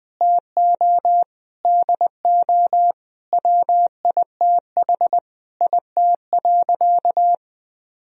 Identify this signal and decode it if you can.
Morse code, heard as a keyed tone, about 20 wpm, 700 Hz. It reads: TO DO WITH IT.
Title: Sentences from Top 1000 Words - Review Part 2 - 20wpm